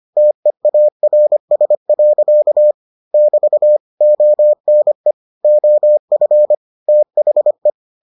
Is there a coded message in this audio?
Yes: Morse code from 0.2 to 7.7 s, heard as a keyed tone, about 25 wpm, 600 Hz. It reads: TEARS. <BT> ONE OF THE